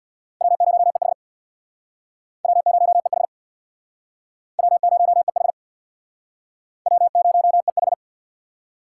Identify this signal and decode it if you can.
Morse code, heard as a keyed tone, about 63 wpm, 700 Hz. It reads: W0EH W0EH W0EH W0EH